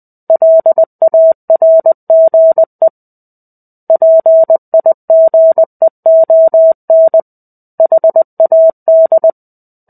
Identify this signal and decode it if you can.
Morse code, heard as a keyed tone, about 20 wpm, 650 Hz. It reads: LARGE PIGEON HAD
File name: 05-Chapter-05 20 WPM